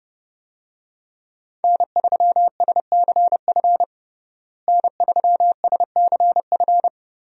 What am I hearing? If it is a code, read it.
Morse code, heard as a keyed tone, about 32 wpm, 700 Hz. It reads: N3SCF N3SCF
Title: USA Call Signs - 5 Characters - Repeated 2x - 30wpm